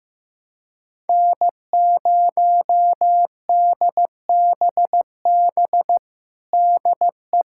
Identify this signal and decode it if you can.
Morse code, heard as a keyed tone, about 16 wpm, 700 Hz. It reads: N0DBB DE
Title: USA Callsign DE Callsign - 5 Characters - 15wpm